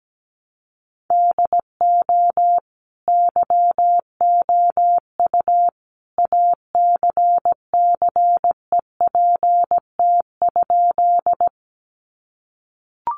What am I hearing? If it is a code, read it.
Morse code, heard as a keyed tone, about 17 wpm, 700 Hz. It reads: DO YOU ACCEPT?
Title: Sentences from Top 1300 Words 17wpm